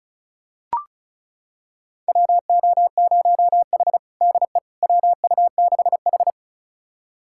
Morse code, heard as a keyed tone, about 34 wpm, 700 Hz. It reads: WO0H DE WU6H
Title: USA Callsign DE Callsign - 4 Characters - 35wpm